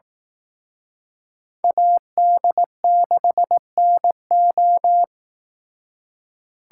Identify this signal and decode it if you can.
Morse code, heard as a keyed tone, about 18 wpm, 700 Hz. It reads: AD6NO
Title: Speed Racing - USA 2x2 Call Signs 15wpm